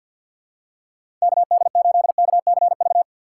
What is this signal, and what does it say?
Morse code, heard as a keyed tone, about 52 wpm, 700 Hz. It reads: KD8KCV
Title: USA Call Signs - 6 Characters - 3x Character Spacing - 50wpm